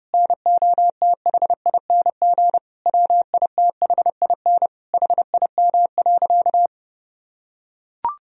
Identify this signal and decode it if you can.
Morse code, heard as a keyed tone, about 30 wpm, 700 Hz. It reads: NOTHING WITHIN HIM.
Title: Sentences from Top 1000 Words - Review Part 4 - 30wpm